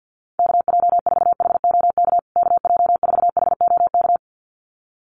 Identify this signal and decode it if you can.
Morse code, heard as a keyed tone, about 50 wpm, 700 Hz. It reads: KJ4HOK KJ4HOK
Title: USA Call Signs Repeated 2x 50wpm